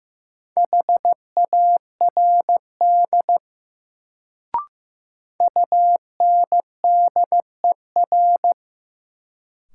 Morse code, heard as a keyed tone, about 15 wpm, 700 Hz. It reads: HARD UNDER
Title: Sentences from Top 300 Words - New Word Drill - 15wpm